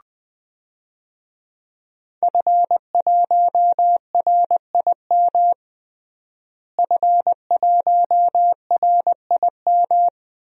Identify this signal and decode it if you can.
Morse code, heard as a keyed tone, about 21 wpm, 700 Hz. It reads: F1RIM F1RIM
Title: DXCC-100 Call Signs - 5 Characters - Repeated 2x - 20wpm